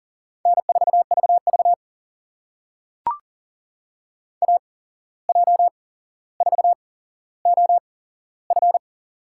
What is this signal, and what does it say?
Morse code, heard as a keyed tone, about 40 wpm, spaced as for 16 wpm, 700 Hz. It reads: N4VV AJ4OF
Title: USA Call Signs - 8x Character Spacing - 40wpm